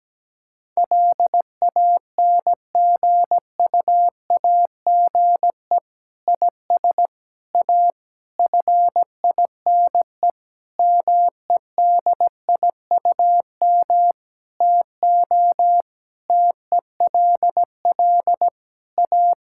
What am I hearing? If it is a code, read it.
Morse code, heard as a keyed tone, about 17 wpm, 700 Hz. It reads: LANGUAGE IS A FINE MEDIUM TO TELL A
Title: Sentences from Top 1000 Words - Review Part 6 - 17wpm